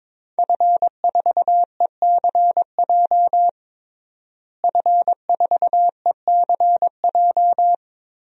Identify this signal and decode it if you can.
Morse code, heard as a keyed tone, about 22 wpm, 700 Hz. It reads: F4ECJ F4ECJ
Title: DXCC-100 Call Signs - 5 Characters - Repeated 2x - 22wpm